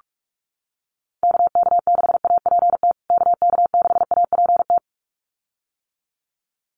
Morse code, heard as a keyed tone, about 44 wpm, 700 Hz. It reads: KK6APT KK6APT
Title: USA Call Signs Repeated 2x 45wpm